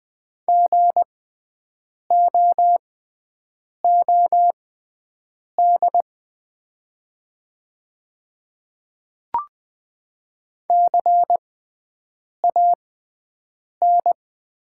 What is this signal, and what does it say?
Morse code, heard as a keyed tone, about 20 wpm, spaced as for 7 wpm, 690 Hz. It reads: GOOD CAN
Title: 6x Character Spacing - Top 300 Words - 20wpm